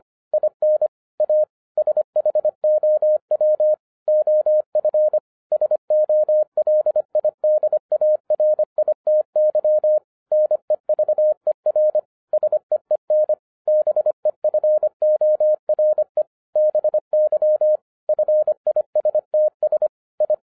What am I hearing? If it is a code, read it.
Morse code, heard as a keyed tone, about 25 wpm, 610 Hz. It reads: IN A SHOW OF SOLIDARITY NEVER SEEN BEFORE BY FISTS I
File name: 2012-01-mp3 / n14-FUN WITH NUMBERS-25wpm